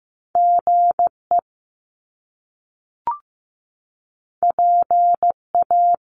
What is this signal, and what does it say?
Morse code, heard as a keyed tone, about 15 wpm, 700 Hz. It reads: GE PA
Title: Top Two Letter Combinations in Words 15wpm